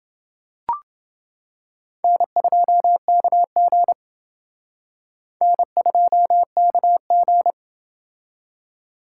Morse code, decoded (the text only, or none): N2KG N2KG